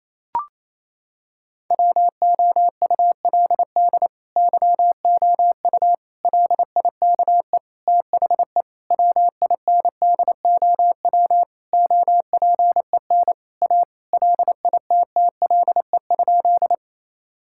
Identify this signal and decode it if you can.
Morse code, heard as a keyed tone, about 28 wpm, 700 Hz. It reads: WOULD YOU LIKE THE WINDOW OPEN A LITTLE?
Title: Sentences from Top 1600 Words 28wpm